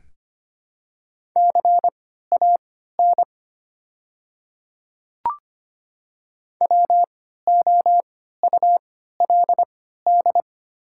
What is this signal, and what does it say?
Morse code, heard as a keyed tone, about 25 wpm, spaced as for 14 wpm, 700 Hz. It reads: CAN WOULD
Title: 3x Character Spacing - Top 100 Words - 25wpm